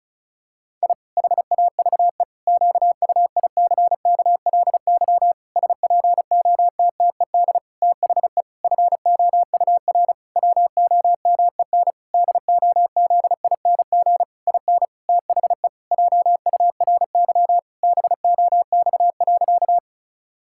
Morse code, heard as a keyed tone, about 34 wpm, 700 Hz. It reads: I HAVE QUICKLY SPOTTED THE FOUR WOMEN DOZING IN THE JURY BOX.
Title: Pangram Challenge v2 - 35wpm